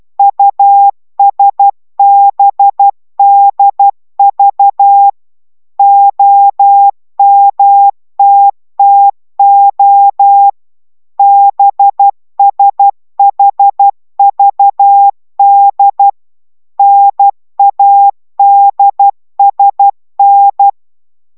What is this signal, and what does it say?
Morse code, heard as a keyed tone, about 12 wpm, 800 Hz. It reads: USBDV OMTTO BSHVD NADSN